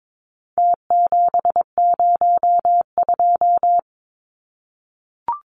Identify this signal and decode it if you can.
Morse code, heard as a keyed tone, about 22 wpm, 700 Hz. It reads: T702
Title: Numbers - Contest Style - 4-Digit 22wpm